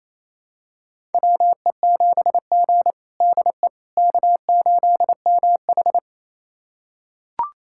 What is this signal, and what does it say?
Morse code, heard as a keyed tone, about 28 wpm, 700 Hz. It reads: WE7G DE K8MH
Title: USA Callsign DE Callsign - 4 Characters - 28wpm